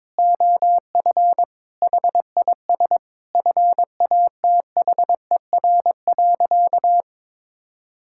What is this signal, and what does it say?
Morse code, heard as a keyed tone, about 22 wpm, 700 Hz. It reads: OF HIS FATHER.
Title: Sentences from Top 1000 Words - Review Part 5 - 22wpm